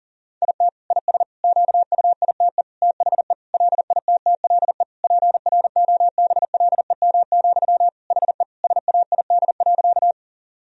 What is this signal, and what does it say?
Morse code, heard as a keyed tone, about 37 wpm, 700 Hz. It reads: IT IS QUITE THE LITTLE PROBLEM, HE SAID.
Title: Sentences from Top 1000 Words - Review Part 2 - 40wpm